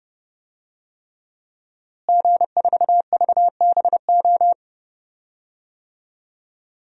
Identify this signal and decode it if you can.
Morse code, heard as a keyed tone, about 30 wpm, 700 Hz. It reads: G4VBO
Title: DXCC-100 Call Signs - 5 Characters - Repeated 3x - 30wpm